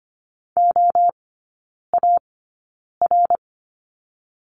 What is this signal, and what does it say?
Morse code, heard as a keyed tone, about 25 wpm, spaced as for 9 wpm, 700 Hz. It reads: OAR
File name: Lesson 14e - Number 2 - Three Characters 25wpm